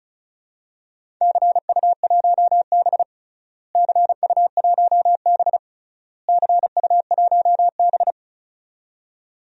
Morse code, heard as a keyed tone, about 36 wpm, 700 Hz. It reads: CU1B CU1B CU1B
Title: DXCC-100 Call Signs - 4 Characters - Repeated 3x - 35wpm